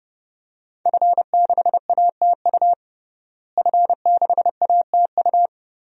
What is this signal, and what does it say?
Morse code, heard as a keyed tone, about 32 wpm, 700 Hz. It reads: F6ATU F6ATU
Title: DXCC-100 Call Signs - 5 Characters - Repeated 3x - 30wpm